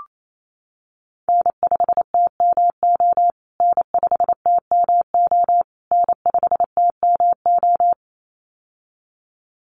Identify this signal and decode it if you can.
Morse code, heard as a keyed tone, about 28 wpm, 700 Hz. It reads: N5TMO N5TMO N5TMO
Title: USA Call Signs Repeated 3x 28wpm